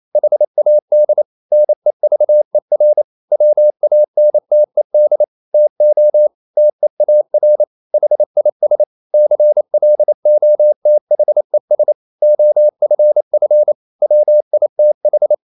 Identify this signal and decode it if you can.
Morse code, heard as a keyed tone, about 28 wpm, 600 Hz. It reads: HAD NEVER WANTED TO TEAR HIS CLOTHES OFF WITH
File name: Never 0027wpm